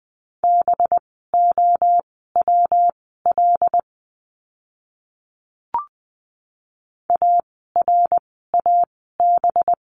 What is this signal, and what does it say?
Morse code, heard as a keyed tone, about 20 wpm, spaced as for 15 wpm, 700 Hz. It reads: BOWL ARAB
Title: 185 Lesson 30f - Letter b - Words 20wpm